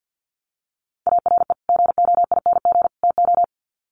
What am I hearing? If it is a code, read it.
Morse code, heard as a keyed tone, about 50 wpm, 700 Hz. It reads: ARE GOING TO
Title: Sets of 3 Words 50wpm